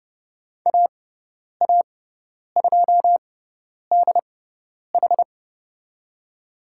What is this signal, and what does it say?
Morse code, heard as a keyed tone, about 30 wpm, spaced as for 10 wpm, 700 Hz. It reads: AA2DH
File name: Lesson 18g - Letter m - Call Signs 30wpm